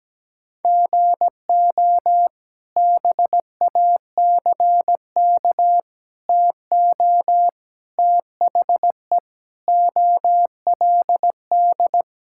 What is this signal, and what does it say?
Morse code, heard as a keyed tone, about 17 wpm, 700 Hz. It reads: GO BACK TO THE OLD
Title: Sentences from Top 1000 Words - Review Part 5 - 17wpm